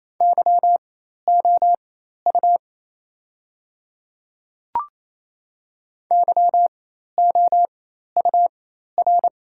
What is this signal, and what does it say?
Morse code, heard as a keyed tone, about 28 wpm, spaced as for 13 wpm, 700 Hz. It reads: YOU YOUR
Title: 4x Character Spacing - Top 500 Words - 28wpm